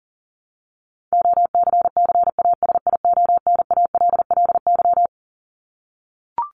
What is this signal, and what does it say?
Morse code, heard as a keyed tone, about 40 wpm, 700 Hz. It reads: OCCASIONALLY
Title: LY Suffix - Top 100 Words with Base Words 40wpm